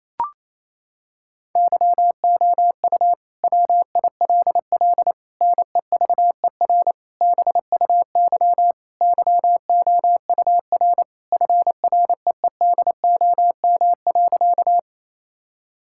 Morse code, heard as a keyed tone, about 28 wpm, 700 Hz. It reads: YOU WILL NEVER BUY YOUR FREEDOM.
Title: Sentences from Top 1000 Words - Review Part 5 - 28wpm